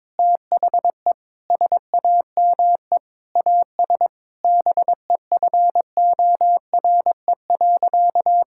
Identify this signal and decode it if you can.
Morse code, heard as a keyed tone, about 22 wpm, 700 Hz. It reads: THE SAME AS BEFORE.
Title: Sentences from Top 1000 Words - Review Part 6 - 22wpm